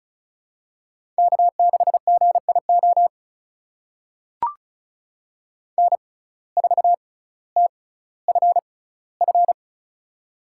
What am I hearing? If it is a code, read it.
Morse code, heard as a keyed tone, about 36 wpm, spaced as for 18 wpm, 700 Hz. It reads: K6GIO N4TFF
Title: USA Call Signs - 5 Characters - 6x Character Spacing - 35wpm